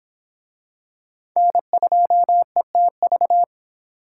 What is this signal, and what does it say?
Morse code, heard as a keyed tone, about 23 wpm, 700 Hz. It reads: N2ETV